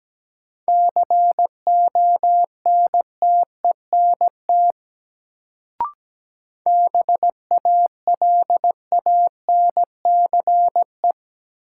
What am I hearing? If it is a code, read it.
Morse code, heard as a keyed tone, about 17 wpm, 700 Hz. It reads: CONTENT BALANCE
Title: Sentences from Top 900 Words - New Word Drill - 17wpm